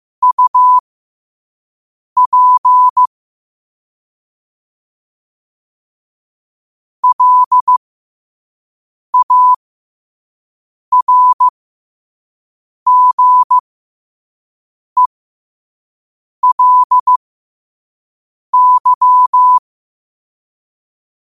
Morse code, heard as a keyed tone, about 15 wpm, spaced as for 5 wpm, 1000 Hz. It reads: UP LARGELY
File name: qotdfiles / 5wpm